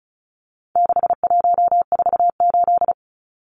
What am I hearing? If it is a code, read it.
Morse code, heard as a keyed tone, about 34 wpm, 700 Hz. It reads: 6148
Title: Numbers - 4-Digit 35wpm